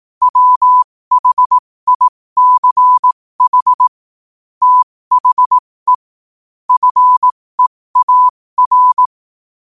Morse code, heard as a keyed tone, about 18 wpm, spaced as for 15 wpm, 1000 Hz. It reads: WHICH THE FEAR
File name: qotdfiles / 15wpm